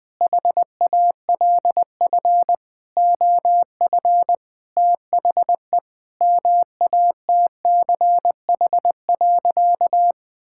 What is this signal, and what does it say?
Morse code, heard as a keyed tone, about 21 wpm, 700 Hz. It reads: HALF OF THE MATCH.
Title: Sentences with 5-Word Phrases - Spoken Beginning - 20wpm